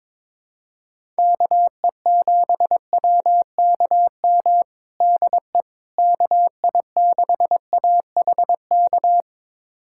Morse code, heard as a keyed tone, about 21 wpm, 700 Hz. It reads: KE7WKM DE KI6AHK
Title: USA Callsign DE Callsign - 6 Characters - 22wpm